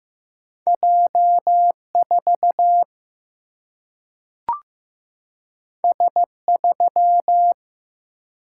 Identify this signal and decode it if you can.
Morse code, heard as a keyed tone, about 15 wpm, 700 Hz. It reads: J 4 S 3
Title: Single-Letter Single-Number 15wpm